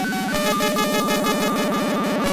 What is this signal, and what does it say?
There is no code.